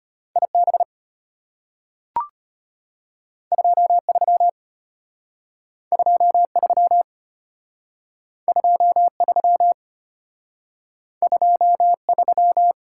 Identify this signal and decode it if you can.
Morse code, heard as a keyed tone, about 33 wpm, 700 Hz. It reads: IB 23 23 23 23